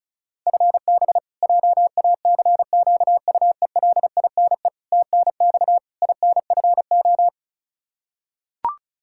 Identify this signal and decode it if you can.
Morse code, heard as a keyed tone, about 36 wpm, 700 Hz. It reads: FB JACQUELINE TNX INFO
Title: Practice QSO - Exchange 3 - More Info - 35wpm